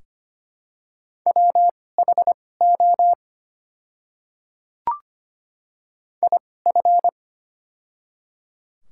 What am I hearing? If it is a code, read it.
Morse code, heard as a keyed tone, about 25 wpm, spaced as for 18 wpm, 700 Hz. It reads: WHO IF
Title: 2x Character Spacing - Top 100 Words - 25wpm